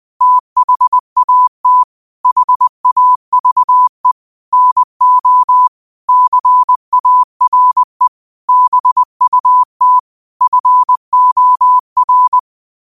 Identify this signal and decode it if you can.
Morse code, heard as a keyed tone, about 20 wpm, 1000 Hz. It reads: THAT HAVE NO CARE BUT FOR
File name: qotdfiles / 20wpm